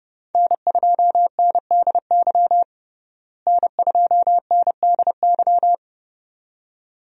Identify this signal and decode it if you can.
Morse code, heard as a keyed tone, about 31 wpm, 700 Hz. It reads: N2NDY N2NDY
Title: USA Call Signs - 5 Characters - Repeated 2x - 30wpm